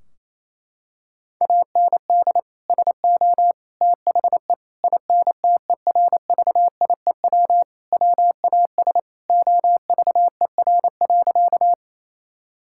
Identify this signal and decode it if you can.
Morse code, heard as a keyed tone, about 28 wpm, 700 Hz. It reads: AND SO THE INTERVIEW WAS OVER.
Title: Sentences from Top 1600 Words 28wpm